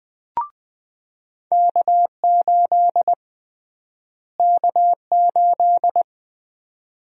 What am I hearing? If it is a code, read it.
Morse code, heard as a keyed tone, about 20 wpm, 700 Hz. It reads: K8 K8
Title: USA Call Sign Prefixes - Repeated 2x - 20wpm